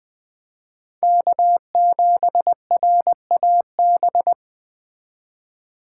Morse code, heard as a keyed tone, about 21 wpm, 700 Hz. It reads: K7RAB